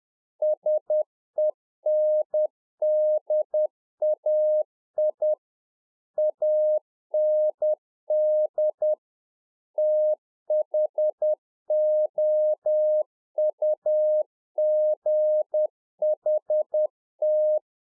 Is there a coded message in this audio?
Yes: Morse code from 0.4 to 17.6 s, heard as a keyed tone, about 10 wpm, 610 Hz. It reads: SENDAI AND THOUGHT